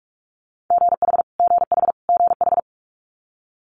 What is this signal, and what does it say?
Morse code, heard as a keyed tone, about 46 wpm, 700 Hz. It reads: GH GH GH
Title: Multi-letter Phonemes Repeated 3x 45wpm